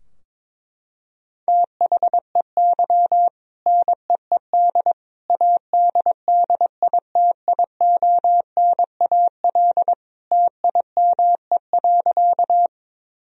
Morse code, heard as a keyed tone, about 21 wpm, 700 Hz. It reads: THEY NEED ADDITIONAL TIME.
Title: Sentences from Top 1000 Words - Review Part 4 - 22wpm